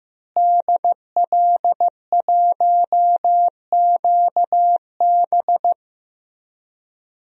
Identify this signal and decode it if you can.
Morse code, heard as a keyed tone, about 15 wpm, 700 Hz. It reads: DL1QB
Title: DXCC-100 Call Signs - 5 Characters - Repeated 2x - 15wpm